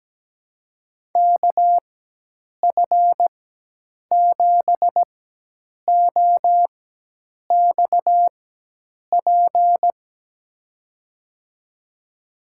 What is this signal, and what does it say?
Morse code, heard as a keyed tone, about 17 wpm, spaced as for 8 wpm, 700 Hz. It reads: KF7OXP
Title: USA Call Signs - 6 Characters - 4x Character Spacing - 17wpm